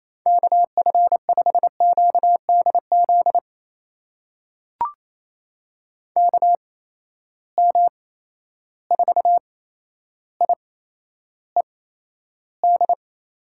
Morse code, heard as a keyed tone, about 28 wpm, spaced as for 12 wpm, 700 Hz. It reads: KF5QDZ KM4IED
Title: USA Call Signs - 6 Characters - 8x Character Spacing - 28wpm